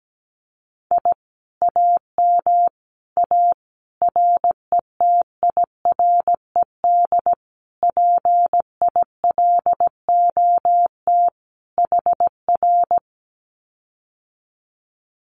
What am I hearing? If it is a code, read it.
Morse code, heard as a keyed tone, about 17 wpm, 700 Hz. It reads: I AM A RETIRED PILOT HR
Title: Common QSO Phrases Encore 17wpm